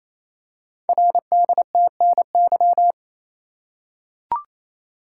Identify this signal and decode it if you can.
Morse code, heard as a keyed tone, about 28 wpm, 700 Hz. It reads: RDTNY
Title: Code Groups - 5 Letters - 28wpm